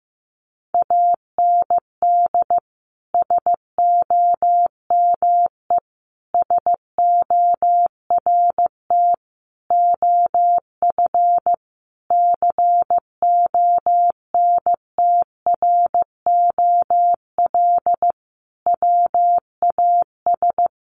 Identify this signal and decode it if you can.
Morse code, heard as a keyed tone, about 14 wpm, 700 Hz. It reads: AND SOME SORT OF CONTROL WAS
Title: Sentences from Top 1300 Words 15wpm